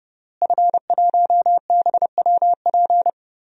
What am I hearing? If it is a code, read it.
Morse code, heard as a keyed tone, about 30 wpm, 700 Hz. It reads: F1BWP
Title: DXCC-100 Call Signs - 5 Characters - Repeated 2x - 30wpm